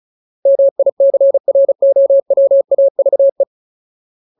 Morse code, heard as a keyed tone, about 34 wpm, 550 Hz. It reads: MICROWAVE
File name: Ultimate Word List / 35WPM Ultimate